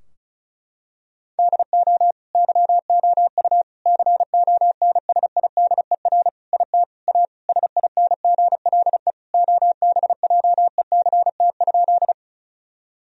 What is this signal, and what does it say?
Morse code, heard as a keyed tone, about 36 wpm, 700 Hz. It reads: DO YOU CONSIDER IT A SINGLE OBJECT?
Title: Sentences from Top 1000 Words - Review Part 4 - 35wpm